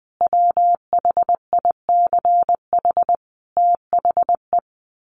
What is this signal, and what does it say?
Morse code, heard as a keyed tone, about 20 wpm, 700 Hz. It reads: WHICH THE
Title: Sets of 5 Words Encore2 20wpm